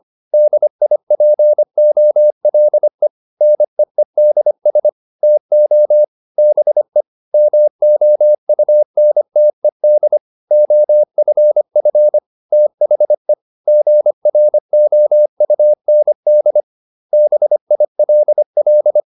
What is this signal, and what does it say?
Morse code, heard as a keyed tone, about 25 wpm, 600 Hz. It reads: DIPOLE NEEDS TO BE MOUNTED OFF THE GROUND BILL